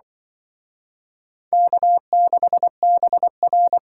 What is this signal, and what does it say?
Morse code, heard as a keyed tone, about 24 wpm, 690 Hz. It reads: K6BR
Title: Speed Racing - USA 1x2 Call Signs 20wpm